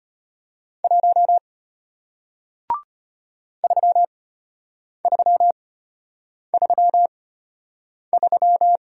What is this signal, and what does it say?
Morse code, heard as a keyed tone, about 34 wpm, spaced as for 8 wpm, 700 Hz. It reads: 1 3333